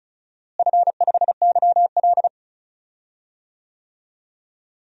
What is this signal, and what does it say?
Morse code, heard as a keyed tone, about 36 wpm, 700 Hz. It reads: F5YL
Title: DXCC-100 Call Signs - 4 Characters - Repeated 2x - 35wpm